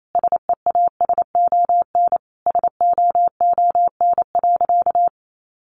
Morse code, heard as a keyed tone, about 28 wpm, 700 Hz. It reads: SEASON SOON.
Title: Sentences from Top 1400 Words 28wpm